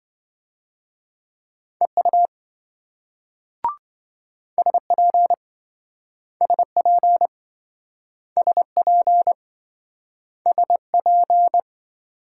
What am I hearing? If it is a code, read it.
Morse code, heard as a keyed tone, about 27 wpm, 700 Hz. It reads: EU SP SP SP SP